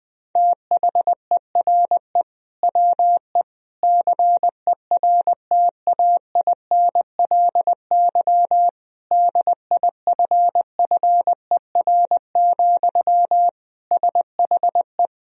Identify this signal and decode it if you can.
Morse code, heard as a keyed tone, about 20 wpm, 700 Hz. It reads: THERE WE CERTAINLY DIFFER, SHE
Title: Sentences from Top 1000 Words - Review Part 1 - 20wpm